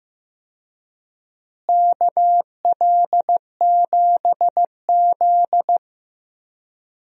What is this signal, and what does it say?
Morse code, heard as a keyed tone, about 15 wpm, 700 Hz. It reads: KL7Z
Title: DXCC-100 Call Signs - 4 Characters - Repeated 2x - 15wpm